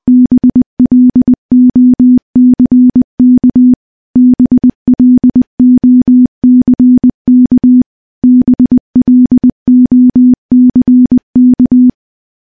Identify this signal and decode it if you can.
Morse code, heard as a keyed tone, about 21 wpm, 250 Hz. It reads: BLOCK BLOCK BLOCK
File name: morse-training-20WPM